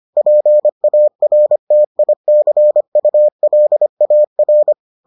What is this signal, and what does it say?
Morse code, heard as a keyed tone, about 25 wpm, 600 Hz. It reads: PARTICULAR